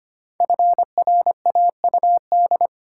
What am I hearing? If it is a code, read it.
Morse code, heard as a keyed tone, about 26 wpm, 700 Hz. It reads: FRAUD
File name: 5 Letter Words - First 3 Letters Given - 25wpm